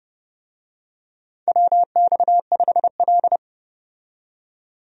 Morse code, heard as a keyed tone, about 30 wpm, 700 Hz. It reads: WX5L
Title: USA Call Signs - 4 Characters - 30wpm